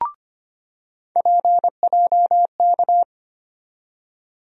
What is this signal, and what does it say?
Morse code, heard as a keyed tone, about 25 wpm, 700 Hz. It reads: PJK
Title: USA Call Sign Postfixes - 3 Letters 25wpm